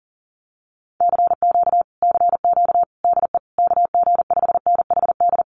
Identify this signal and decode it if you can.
Morse code, heard as a keyed tone, about 40 wpm, 700 Hz. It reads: CQ CQ DE KG5NHD
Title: Common QSO Phrases Encore 40wpm